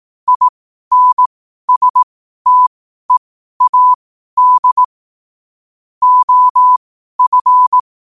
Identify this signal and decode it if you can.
Morse code, heard as a keyed tone, about 18 wpm, spaced as for 12 wpm, 1000 Hz. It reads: INSTEAD OF
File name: qotdfiles / 12wpm